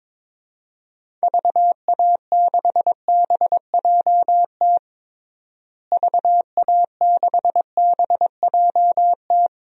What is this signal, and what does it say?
Morse code, heard as a keyed tone, about 22 wpm, 700 Hz. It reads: VA6BJT VA6BJT
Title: DXCC-100 Call Signs - 6 Characters - Repeated 2x - 22wpm